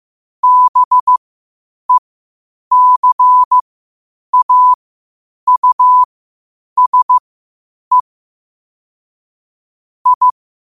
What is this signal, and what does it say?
Morse code, heard as a keyed tone, about 15 wpm, spaced as for 8 wpm, 1000 Hz. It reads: BECAUSE I
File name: qotdfiles / 8wpm